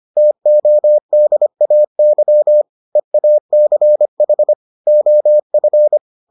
Morse code, heard as a keyed tone, about 25 wpm, 600 Hz. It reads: TODAY EACH OF